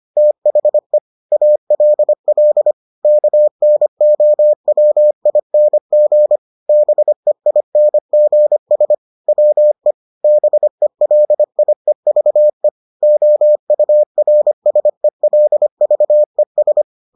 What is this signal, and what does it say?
Morse code, heard as a keyed tone, about 25 wpm, 600 Hz. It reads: THE ALL KNOWING BEINGS WE BELIEVE OURSELVES